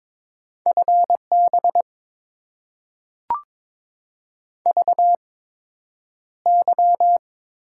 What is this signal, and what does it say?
Morse code, heard as a keyed tone, about 22 wpm, spaced as for 9 wpm, 700 Hz. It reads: FB VY
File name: Top 90 QSO Elements - 8x Character Spacing - 22wpm